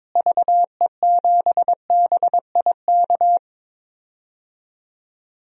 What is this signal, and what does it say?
Morse code, heard as a keyed tone, about 22 wpm, 700 Hz. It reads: VE7BIK
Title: DXCC-100 Call Signs - 6 Characters - Repeated 3x - 22wpm